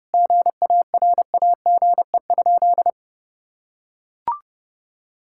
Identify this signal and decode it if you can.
Morse code, heard as a keyed tone, about 30 wpm, 700 Hz. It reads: GARAGE?
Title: Sentences from Top 1600 Words 30wpm